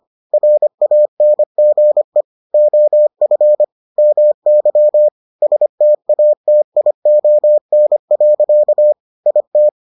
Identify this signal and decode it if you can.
Morse code, heard as a keyed tone, about 25 wpm, 600 Hz. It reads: RANGE OF MY STATION. IT